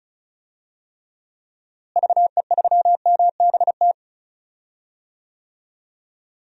Morse code, heard as a keyed tone, about 36 wpm, 700 Hz. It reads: VE3MBT
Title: DXCC-100 Call Signs - 6 Characters - Repeated 3x - 35wpm